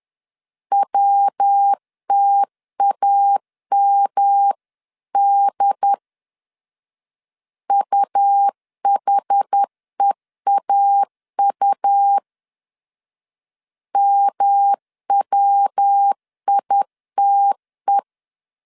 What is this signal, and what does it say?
Morse code, heard as a keyed tone, about 10 wpm, 800 Hz. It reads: WTAMD UHEAU MWITE
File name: CW / DET / 1 3 1500C S C 10wpm 3335